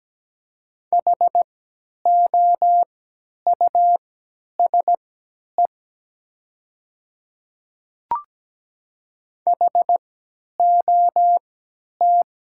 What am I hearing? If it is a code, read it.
Morse code, heard as a keyed tone, about 17 wpm, spaced as for 10 wpm, 700 Hz. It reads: HOUSE HOT
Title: 3x Character Spacing - Top 300 Words - 17wpm